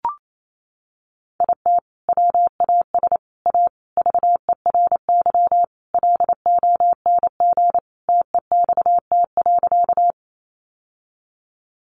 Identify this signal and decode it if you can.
Morse code, heard as a keyed tone, about 28 wpm, 700 Hz. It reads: IT WAS A VERY LONG TEXT.
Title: Sentences from Top 1200 Words 28wpm